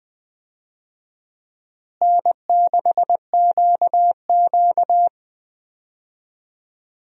Morse code, heard as a keyed tone, about 20 wpm, 700 Hz. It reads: N6QQ
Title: USA Call Signs - 4 Characters - Repeated 2x - 20wpm